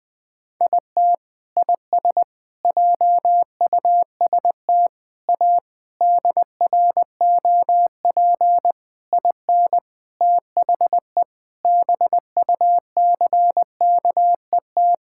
Morse code, heard as a keyed tone, about 20 wpm, 700 Hz. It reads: IT IS JUST A DROP IN THE BUCKET
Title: Sentences with 8-Word Phrases - Spoken Beginning - 20wpm